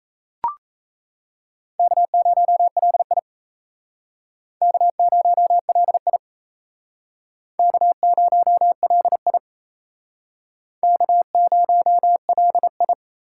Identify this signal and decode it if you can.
Morse code, heard as a keyed tone, about 35 wpm, 700 Hz. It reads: K0LI K0LI K0LI K0LI
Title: Speed Racing - USA 1x2 Call Signs 28wpm